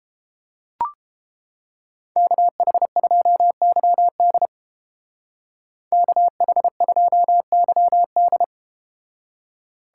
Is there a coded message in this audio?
Yes: Morse code from 2.2 to 8.4 s, heard as a keyed tone, about 31 wpm, 700 Hz. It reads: KH2YD KH2YD